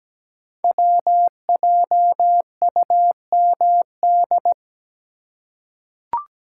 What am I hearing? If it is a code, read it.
Morse code, heard as a keyed tone, about 17 wpm, 700 Hz. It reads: WJUMD
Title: Code Groups - 5 Letters - 17wpm